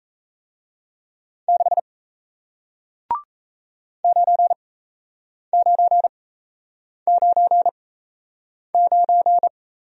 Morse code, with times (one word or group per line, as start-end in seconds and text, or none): 1.5-1.8 6
4.0-9.5 9999